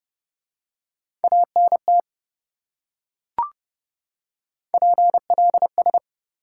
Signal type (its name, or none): Morse code